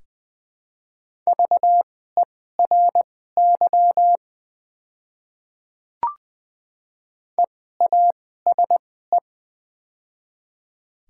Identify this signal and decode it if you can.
Morse code, heard as a keyed tone, about 20 wpm, spaced as for 14 wpm, 690 Hz. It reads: VERY EASE
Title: 2x Character Spacing - Top 500 Words - 20wpm